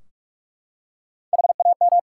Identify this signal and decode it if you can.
Morse code, heard as a keyed tone, about 46 wpm, 700 Hz. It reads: HAM
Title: Practice QSO - Exchange 3 - More Info - 45wpm